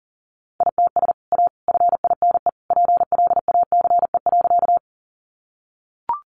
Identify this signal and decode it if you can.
Morse code, heard as a keyed tone, about 40 wpm, 700 Hz. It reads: ITS A FINE PLACE.
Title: Sentences from Top 600 Words 40wpm